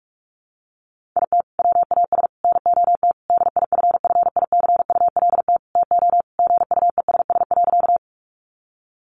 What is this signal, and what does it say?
Morse code, heard as a keyed tone, about 44 wpm, 700 Hz. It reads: IT WAS NOT DIFFICULT TO GUESS.
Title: Sentences from Top 1500 Words 45wpm